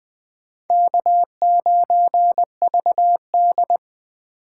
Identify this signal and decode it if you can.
Morse code, heard as a keyed tone, about 20 wpm, 700 Hz. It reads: K9VD
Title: 2023 SOTA Chaser Call Signs - 20wpm